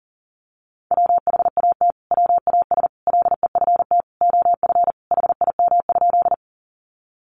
Morse code, heard as a keyed tone, about 40 wpm, 700 Hz. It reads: WHAT WAS LEFT OF HIM?
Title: Sentences from Top 300 Words 40wpm